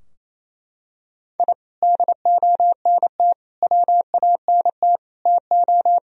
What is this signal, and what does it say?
Morse code, heard as a keyed tone, about 28 wpm, 700 Hz. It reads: I DONT WANT TO
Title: Sentences from Top 1000 Words - Review Part 4 - 28wpm